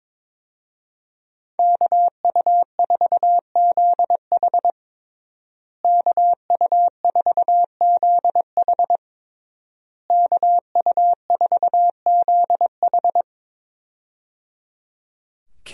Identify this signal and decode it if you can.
Morse code, heard as a keyed tone, about 22 wpm, 700 Hz. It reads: KU4ZH KU4ZH KU4ZH
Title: USA Call Signs - 5 Characters - Repeated 3x - 22wpm